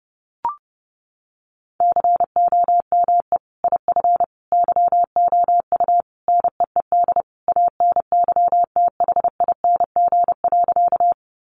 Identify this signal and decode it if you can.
Morse code, heard as a keyed tone, about 30 wpm, 700 Hz. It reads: COME IF YOU NEED ANYTHING.
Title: Sentences from Top 1000 Words 30wpm